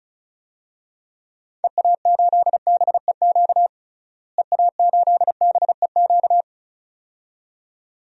Morse code, heard as a keyed tone, about 34 wpm, 700 Hz. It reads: EA8BEQ EA8BEQ
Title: DXCC-100 Call Signs - 6 Characters - Repeated 2x - 35wpm